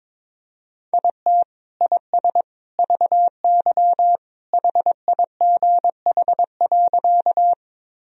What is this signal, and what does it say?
Morse code, heard as a keyed tone, about 22 wpm, 700 Hz. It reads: IT IS VY HIGH.
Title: Sentences from Top 200 Words and Abbreviations 22wpm